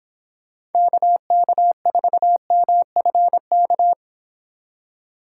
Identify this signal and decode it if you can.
Morse code, heard as a keyed tone, about 30 wpm, 700 Hz. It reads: KK4MFK